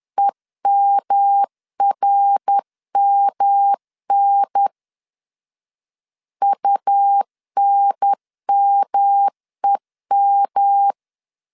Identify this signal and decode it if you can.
Morse code, heard as a keyed tone, about 10 wpm, 800 Hz. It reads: EMRMN UNMEM
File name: CW / DET / 1 3 1500C S B 10wpm 3335